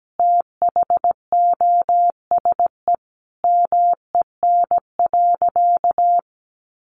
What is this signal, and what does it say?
Morse code, heard as a keyed tone, about 17 wpm, 700 Hz. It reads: THOSE MEN.
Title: Sentences from Top 300 Words 17wpm